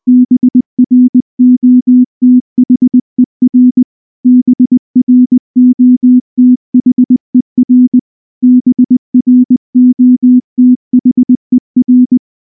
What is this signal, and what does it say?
Morse code, heard as a keyed tone, about 19 wpm, 250 Hz. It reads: BROTHER BROTHER BROTHER
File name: morse / morse-training-20WPM-6